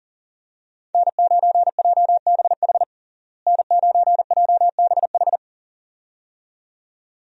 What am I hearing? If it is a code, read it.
Morse code, heard as a keyed tone, about 40 wpm, 700 Hz. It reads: N9JBH N9JBH
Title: USA Call Signs - 5 Characters - Repeated 2x - 40wpm